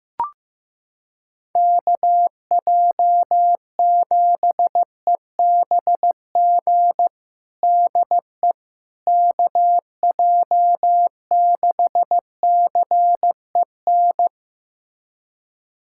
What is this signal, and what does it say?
Morse code, heard as a keyed tone, about 15 wpm, 700 Hz. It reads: KJ7EBG DE KJ6CEN
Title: USA Callsign DE Callsign - 6 Characters - 15wpm